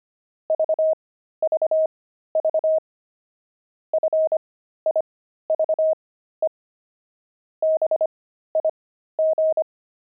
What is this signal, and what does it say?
Morse code, heard as a keyed tone, about 25 wpm, spaced as for 13 wpm, 630 Hz. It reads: VVV FIVE BIG
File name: pangram-fs-13-cs-25-five